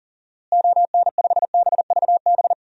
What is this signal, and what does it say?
Morse code, heard as a keyed tone, about 40 wpm, 700 Hz. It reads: ON5BVB
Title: DXCC-100 Call Signs - 6 Characters - Repeated 3x - 40wpm